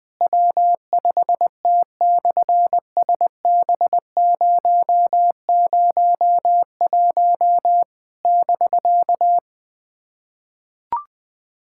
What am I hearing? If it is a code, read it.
Morse code, heard as a keyed tone, about 20 wpm, 700 Hz. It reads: W5T/SB001 <BK>
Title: SOTA QSO Practice 20wpm